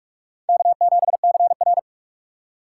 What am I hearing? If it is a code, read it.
Morse code, heard as a keyed tone, about 42 wpm, 700 Hz. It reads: K7CR
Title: USA Call Signs - 4 Characters - 2x Character Spacing - 45wpm